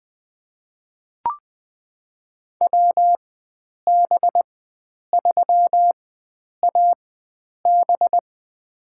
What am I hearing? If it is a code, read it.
Morse code, heard as a keyed tone, about 20 wpm, spaced as for 9 wpm, 700 Hz. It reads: WB3AB
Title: USA Call Signs - 4x Character Spacing - 20wpm